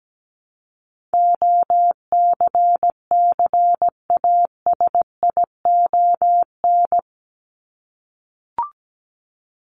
Morse code, heard as a keyed tone, about 17 wpm, 700 Hz. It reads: OCCASION
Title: Top 1501-1750 Words 17wpm